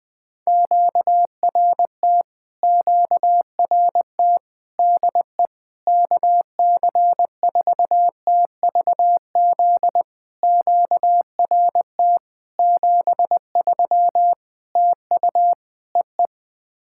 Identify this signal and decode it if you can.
Morse code, heard as a keyed tone, about 20 wpm, 700 Hz. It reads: QRT QRT DE KC4TVZ QRT 73 TU EE
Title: POTA QSO Practice - 20wpm